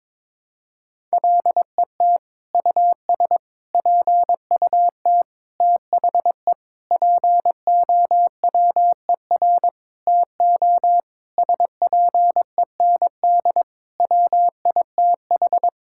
Morse code, heard as a keyed tone, about 22 wpm, 700 Hz. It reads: LET US PUT THE POWER TO SPEND WITH
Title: Sentences from Top 1000 Words - Review Part 5 - 22wpm